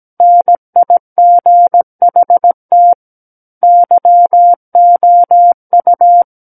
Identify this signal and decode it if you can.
Morse code, heard as a keyed tone, about 17 wpm, 700 Hz. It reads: NIGHT YOU
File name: Book One / 08-Chapter 8 - 17 WPM